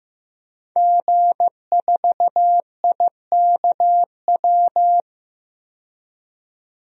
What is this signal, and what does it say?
Morse code, heard as a keyed tone, about 15 wpm, 700 Hz. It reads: G4IKW
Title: DXCC-100 Call Signs - 5 Characters - 15wpm